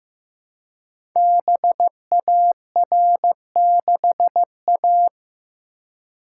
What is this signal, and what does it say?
Morse code, heard as a keyed tone, about 15 wpm, 700 Hz. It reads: BAR6A
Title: Code Groups - 5 Letter:Numbers - 15wpm